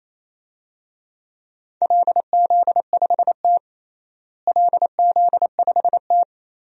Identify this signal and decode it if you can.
Morse code, heard as a keyed tone, about 27 wpm, 700 Hz. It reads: LZ5T LZ5T
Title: DXCC-100 Call Signs - 4 Characters - Repeated 2x - 28wpm